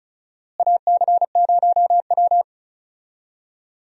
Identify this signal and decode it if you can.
Morse code, heard as a keyed tone, about 34 wpm, 700 Hz. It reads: AC0W
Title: USA Call Signs - 4 Characters - Repeated 3x - 35wpm